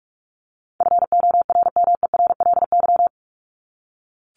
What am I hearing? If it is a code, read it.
Morse code, heard as a keyed tone, about 44 wpm, 700 Hz. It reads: FORMERLY
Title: LY Suffix - Top 100 Words with Base Words 45wpm